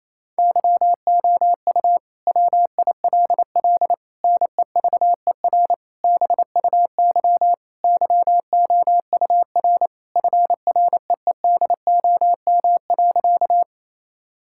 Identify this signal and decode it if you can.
Morse code, heard as a keyed tone, about 28 wpm, 700 Hz. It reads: YOU WILL NEVER BUY YOUR FREEDOM.
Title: Sentences from Top 1000 Words - Review Part 5 - 28wpm